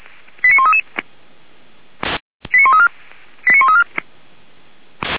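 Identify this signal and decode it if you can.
Five-tone selective call (ZVEI), heard as quick runs of five tones, tones of about 70 ms. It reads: 89133 89134 89134
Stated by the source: ZVEI five-tone selective call received by SDR, fire brigade Orlamünde (Feuerwehr Orlamünde).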